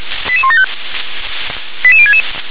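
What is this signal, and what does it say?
Five-tone selective call (ZVEI), heard as quick runs of five tones, tones of about 80 ms. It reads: 88175 88066
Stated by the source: ZVEI five-tone call 88105 received on SDR.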